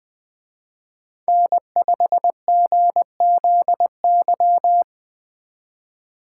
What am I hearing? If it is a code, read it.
Morse code, heard as a keyed tone, about 20 wpm, 700 Hz. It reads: N5GZY
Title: Speed Racing - USA 1x3 Call Signs 17wpm